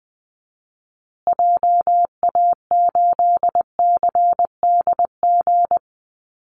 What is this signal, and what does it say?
Morse code, heard as a keyed tone, about 19 wpm, 700 Hz. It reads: JA8CDG
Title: Call Signs Repeated 2x - 20wpm